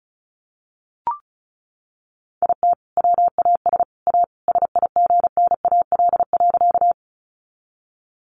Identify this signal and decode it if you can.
Morse code, heard as a keyed tone, about 34 wpm, 700 Hz. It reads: IT WAS A SIGNAL.
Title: Sentences from Top 1400 Words 35wpm